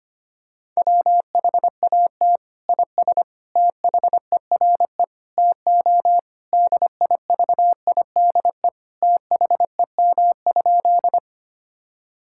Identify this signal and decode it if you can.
Morse code, heard as a keyed tone, about 25 wpm, 700 Hz. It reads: WHAT IS THERE TO DIVIDE THEM?
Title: Sentences from Top 1600 Words 25wpm